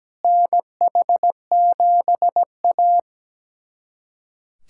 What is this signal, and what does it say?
Morse code, heard as a keyed tone, about 17 wpm, 700 Hz. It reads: NH7A
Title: USA Callsign DE Callsign - 4 Characters - 17wpm